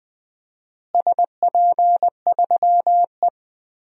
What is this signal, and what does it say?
Morse code, heard as a keyed tone, about 20 wpm, 700 Hz. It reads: SP3E
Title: DXCC-100 Call Signs - 4 Characters - Repeated 2x - 20wpm